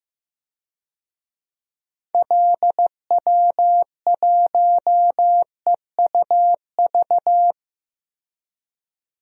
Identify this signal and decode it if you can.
Morse code, heard as a keyed tone, about 16 wpm, 700 Hz. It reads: LW1EUV
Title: DXCC-100 Call Signs - 6 Characters - 15wpm